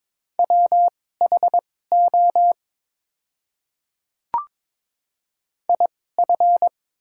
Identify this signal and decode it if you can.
Morse code, heard as a keyed tone, about 22 wpm, spaced as for 16 wpm, 700 Hz. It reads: WHO IF
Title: 2x Character Spacing - Top 100 Words - 22wpm